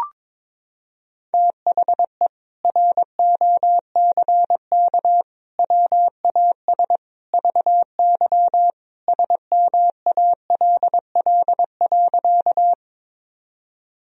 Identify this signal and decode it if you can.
Morse code, heard as a keyed tone, about 22 wpm, 700 Hz. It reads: THE ROCK WAS VY SMALL.
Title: Sentences from Top 400 Words and Abbreviations 22wpm